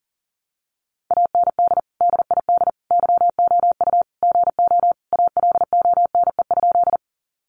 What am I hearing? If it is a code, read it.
Morse code, heard as a keyed tone, about 40 wpm, 700 Hz. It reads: AND DID YOU GO ALONE?
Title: Sentences from Top 1000 Words 40wpm